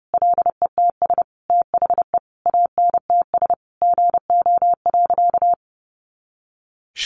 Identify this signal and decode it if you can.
Morse code, heard as a keyed tone, about 28 wpm, 700 Hz. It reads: LETS THE ANTS GO.
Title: Additional Easy Phrases 30wpm